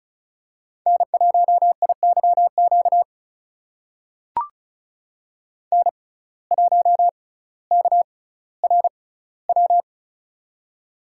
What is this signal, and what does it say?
Morse code, heard as a keyed tone, about 34 wpm, spaced as for 18 wpm, 700 Hz. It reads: N1IYQ N1KRW
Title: USA Call Signs - 6x Character Spacing - 35wpm